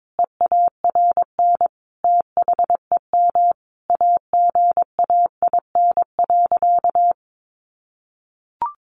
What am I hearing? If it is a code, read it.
Morse code, heard as a keyed tone, about 22 wpm, 700 Hz. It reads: EARN THEM AGAIN.
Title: Sentences from Top 1500 Words 22wpm